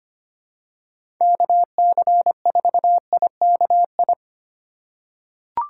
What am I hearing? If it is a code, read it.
Morse code, heard as a keyed tone, about 25 wpm, 700 Hz. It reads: KC4IKI
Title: USA Call Signs - 8x Character Spacing - 25wpm